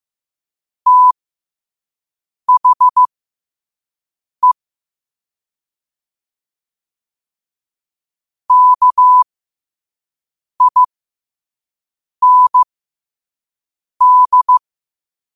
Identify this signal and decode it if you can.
Morse code, heard as a keyed tone, about 15 wpm, spaced as for 5 wpm, 1000 Hz. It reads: THE KIND
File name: qotdfiles / 5wpm